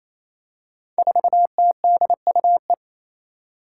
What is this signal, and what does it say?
Morse code, heard as a keyed tone, about 27 wpm, 700 Hz. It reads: 4TDUE
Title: Five Challenging Letter-Number Combinations 28wpm